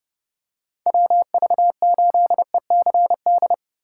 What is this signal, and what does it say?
Morse code, heard as a keyed tone, about 30 wpm, 700 Hz. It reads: WV8ECD
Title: USA Call Signs - 6 Characters - 6x Character Spacing - 30wpm